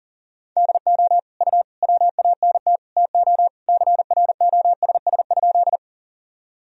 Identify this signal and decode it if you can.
Morse code, heard as a keyed tone, about 40 wpm, 700 Hz. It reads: DO U WANT TO CROSS?
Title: Sentences from Top 300 Words and Abbreviations 40wpm